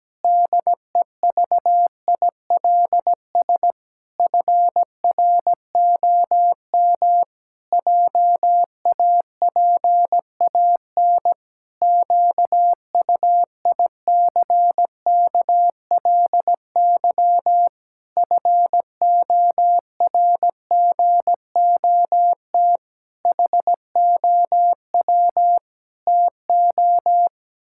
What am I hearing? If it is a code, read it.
Morse code, heard as a keyed tone, about 17 wpm, 700 Hz. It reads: DEVILS FROM JAPAN QUICKLY FORGOT HOW TO
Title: Pangram Challenge v2 - 17wpm